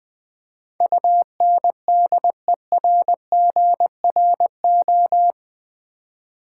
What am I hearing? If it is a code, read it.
Morse code, heard as a keyed tone, about 19 wpm, 700 Hz. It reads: UNDERGRO
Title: Build A Word - 4 Letters At A Time - 11-Letter Words - 20wpm